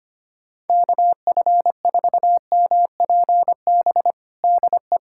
Morse code, heard as a keyed tone, about 25 wpm, 700 Hz. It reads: KF4MPB DE
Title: USA Callsign DE Callsign 25wpm